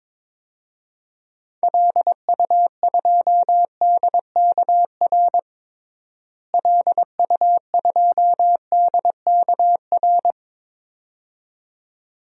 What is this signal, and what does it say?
Morse code, heard as a keyed tone, about 22 wpm, 700 Hz. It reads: LU2DKR LU2DKR
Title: DXCC-100 Call Signs - 6 Characters - Repeated 2x - 22wpm